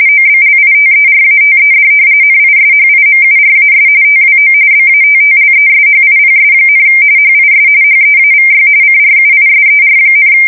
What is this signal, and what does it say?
RTTY, about 45.45 baud. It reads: OBLITERATED NISHIMURAS TWO BATTLESHIPS, HIS HEAVY CRUISER,
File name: fsk170 / battleshipnj